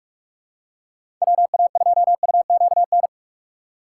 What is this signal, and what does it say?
Morse code, heard as a keyed tone, about 46 wpm, 700 Hz. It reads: WA2UQN
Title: USA Call Signs - 6 Characters - 2x Character Spacing - 45wpm